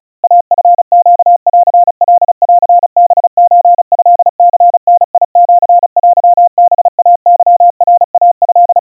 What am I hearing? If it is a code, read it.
Morse code (Japanese Wabun code), heard as a keyed tone, about 36 wpm, 700 Hz. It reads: イチネンカンハソトニダシテハイケナイト
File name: kawa / Morse sample / kamyu 130 cpm 700 Hz Crispy